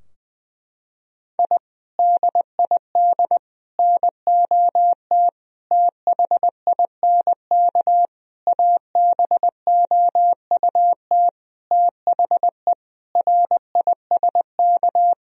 Morse code, heard as a keyed tone, about 20 wpm, 700 Hz. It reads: I DID NOT THINK ABOUT THE RISK
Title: Sentences from Top 1000 Words - Review Part 4 - 20wpm